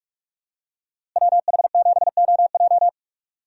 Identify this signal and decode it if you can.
Morse code, heard as a keyed tone, about 43 wpm, 700 Hz. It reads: WH7OJ
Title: USA Call Signs - 5 Characters - Repeated 3x - 45wpm